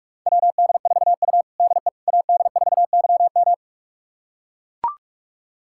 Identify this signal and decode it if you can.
Morse code, heard as a keyed tone, about 42 wpm, 700 Hz. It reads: WD4U DE AD4YM
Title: USA Callsign DE Callsign 45wpm